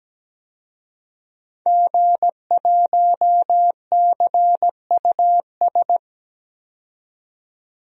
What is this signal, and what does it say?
Morse code, heard as a keyed tone, about 17 wpm, 700 Hz. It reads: G1CUS
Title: DXCC-100 Call Signs - 5 Characters - 17wpm